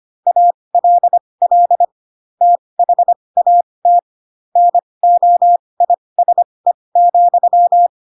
Morse code, heard as a keyed tone, about 25 wpm, spaced as for 20 wpm, 700 Hz. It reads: ALL THAT NOISE,